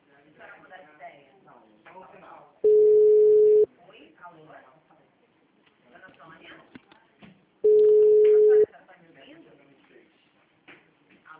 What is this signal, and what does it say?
Ringback tone (Europe, 425 Hz), with speech.